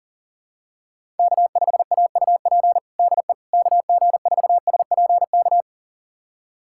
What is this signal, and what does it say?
Morse code, heard as a keyed tone, about 40 wpm, 700 Hz. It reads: K5AUP DE KG4SPK
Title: USA Callsign DE Callsign 40wpm